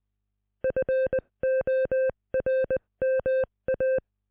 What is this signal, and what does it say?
Morse code, heard as a keyed tone, about 20 wpm, spaced as for 18 wpm, 520 Hz. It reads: FORMA